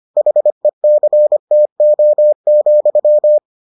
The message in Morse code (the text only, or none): HECTO,